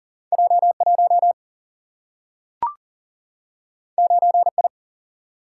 Morse code, heard as a keyed tone, about 40 wpm, 700 Hz. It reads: J1 9I
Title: Single-Letter Single-Number 40wpm